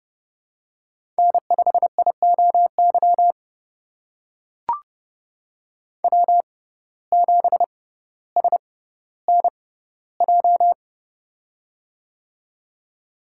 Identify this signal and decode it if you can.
Morse code, heard as a keyed tone, about 30 wpm, spaced as for 15 wpm, 700 Hz. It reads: N5IOY W7SNJ
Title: USA Call Signs - 5 Characters - 6x Character Spacing - 30wpm